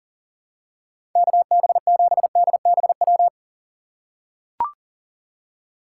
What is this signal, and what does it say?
Morse code, heard as a keyed tone, about 40 wpm, 700 Hz. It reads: KB7DBW
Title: USA Call Signs - 6 Characters - 3x Character Spacing - 40wpm